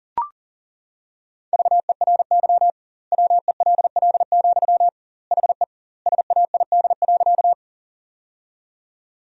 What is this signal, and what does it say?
Morse code, heard as a keyed tone, about 40 wpm, 700 Hz. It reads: VERY WELL, HE SAID.
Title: 2x Word-Spacing - Sentences from Top 200 Words - 40wpm